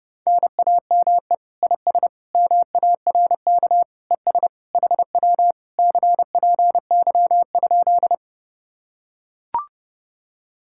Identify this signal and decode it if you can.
Morse code, heard as a keyed tone, about 29 wpm, 700 Hz. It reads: NAME IS MARK ES HW CPY?